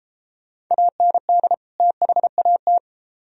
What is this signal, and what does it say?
Morse code, heard as a keyed tone, about 32 wpm, 700 Hz. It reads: AND THAT